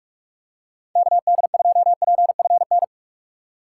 Morse code, heard as a keyed tone, about 43 wpm, 700 Hz. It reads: KD2PFN
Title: USA Call Signs - 6 Characters - 2x Character Spacing - 45wpm